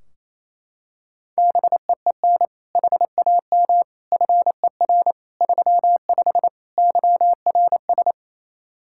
Morse code, heard as a keyed tone, about 28 wpm, 700 Hz. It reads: BEEN HAM FER 35 YRS
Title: Practice QSO - Exchange 3 - More Info - 28wpm